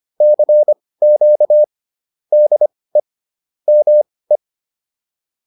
Morse code, heard as a keyed tone, about 25 wpm, spaced as for 18 wpm, 600 Hz. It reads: CQ DE ME
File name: PR103 18